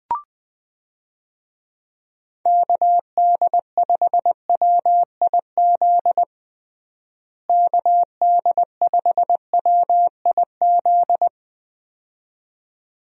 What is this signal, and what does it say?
Morse code, heard as a keyed tone, about 20 wpm, 700 Hz. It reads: KD5WIZ KD5WIZ
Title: USA Call Signs - 6 Characters - Repeated 2x - 20wpm